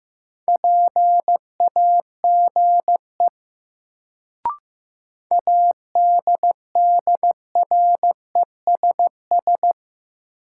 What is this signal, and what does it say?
Morse code, heard as a keyed tone, about 15 wpm, 700 Hz. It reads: PAGE ADDRESS
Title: Sets of 5 Words Encore - New Word Drill - 15wpm